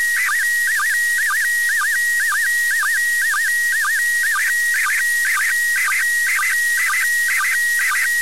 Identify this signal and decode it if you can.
SSTV (slow-scan television), PD 120.
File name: ISS-SSTV-audio 145802924Hz